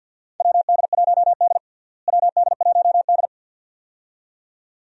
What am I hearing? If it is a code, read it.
Morse code, heard as a keyed tone, about 47 wpm, 700 Hz. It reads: WD1D WD1D
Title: USA Call Signs - 4 Characters - Repeated 2x - 50wpm